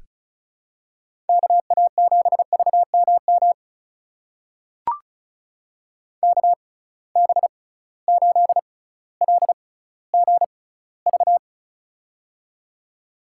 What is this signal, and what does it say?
Morse code, heard as a keyed tone, about 36 wpm, spaced as for 18 wpm, 700 Hz. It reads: KA7VMM KB8LGV
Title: USA Call Signs - 6 Characters - 6x Character Spacing - 35wpm